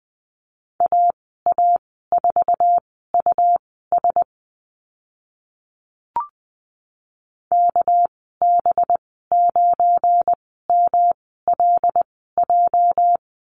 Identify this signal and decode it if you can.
Morse code, heard as a keyed tone, about 20 wpm, spaced as for 15 wpm, 700 Hz. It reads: AA4US KB9MLJ
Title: 207 Lesson 33g - Letter j - Call Signs 20wpm